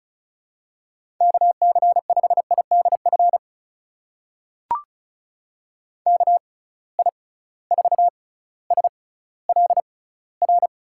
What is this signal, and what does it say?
Morse code, heard as a keyed tone, about 36 wpm, spaced as for 18 wpm, 700 Hz. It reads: KC5IDF KI4SLR
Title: USA Call Signs - 6 Characters - 6x Character Spacing - 35wpm